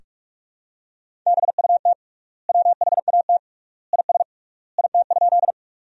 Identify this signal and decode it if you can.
Morse code, heard as a keyed tone, about 44 wpm, 700 Hz. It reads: BUT WHAT IS IT?
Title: 3x Word-Spacing - Sentences from Top 100 Words - 45wpm